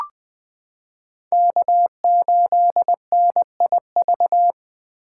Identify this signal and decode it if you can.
Morse code, heard as a keyed tone, about 20 wpm, 700 Hz. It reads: K8NIV
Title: 2023 POTA Hunter Call Signs - 20wpm